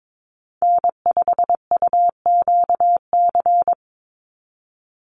Morse code, heard as a keyed tone, about 22 wpm, 700 Hz. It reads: N5UQC
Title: USA Call Signs 22wpm